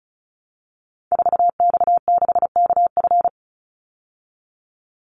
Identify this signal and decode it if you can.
Morse code, heard as a keyed tone, about 34 wpm, 700 Hz. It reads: 4X6KF
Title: Call Signs Repeated 3x - 35wpm